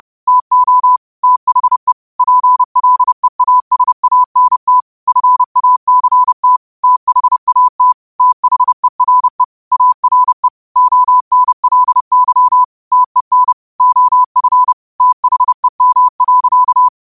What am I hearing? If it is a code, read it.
Morse code, heard as a keyed tone, about 30 wpm, 1000 Hz. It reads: TO THE PLEASANT FACT THAT THERE ARE ONLY TEN OF THEM.